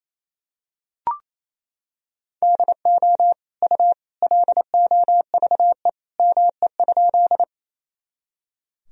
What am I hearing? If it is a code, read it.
Morse code, heard as a keyed tone, about 28 wpm, 700 Hz. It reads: DO U LOVE ME?
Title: Sentences from Top 400 Words and Abbreviations 28wpm